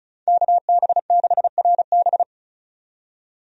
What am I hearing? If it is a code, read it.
Morse code, heard as a keyed tone, about 34 wpm, 700 Hz. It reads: KB6RB
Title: USA Call Signs - 5 Characters - 35wpm